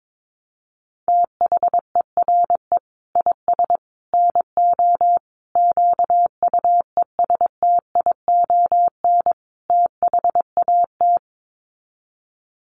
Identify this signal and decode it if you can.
Morse code, heard as a keyed tone, about 22 wpm, 700 Hz. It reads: THERE IS NO QUESTION THAT
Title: Sets of 5 Words Encore 22wpm